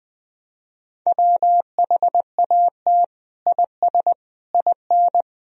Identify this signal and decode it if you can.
Morse code, heard as a keyed tone, about 20 wpm, 700 Hz. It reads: WHAT IS IN
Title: Sentences from Top 200 Words and Abbreviations 20wpm